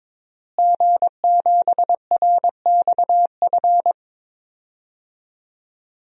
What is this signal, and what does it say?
Morse code, heard as a keyed tone, about 22 wpm, 700 Hz. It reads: G7RXF
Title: DXCC-100 Call Signs - 5 Characters - Repeated 2x - 22wpm